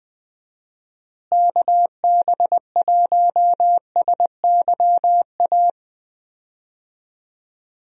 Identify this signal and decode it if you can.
Morse code, heard as a keyed tone, about 21 wpm, 700 Hz. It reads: KB1SYA
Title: USA Call Signs - 6 Characters - Repeated 2x - 20wpm